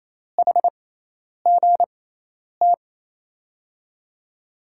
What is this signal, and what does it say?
Morse code, heard as a keyed tone, about 28 wpm, spaced as for 10 wpm, 700 Hz. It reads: HGT
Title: USA Call Signs - 6x Character Spacing - 28wpm